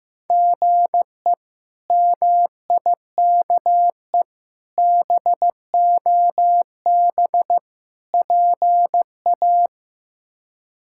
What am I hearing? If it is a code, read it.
Morse code, heard as a keyed tone, about 15 wpm, 700 Hz. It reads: GE MIKE BOB PA
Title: K1USN SST QSO Practice - 15wpm